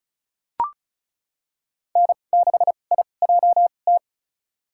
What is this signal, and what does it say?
Morse code, heard as a keyed tone, about 34 wpm, spaced as for 25 wpm, 700 Hz. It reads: N6IJT
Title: USA Call Signs - 5 Characters - 2x Character Spacing - 35wpm